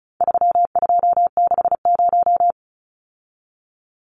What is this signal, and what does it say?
Morse code, heard as a keyed tone, about 34 wpm, 700 Hz. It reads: 3260
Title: Numbers - 4-Digit 35wpm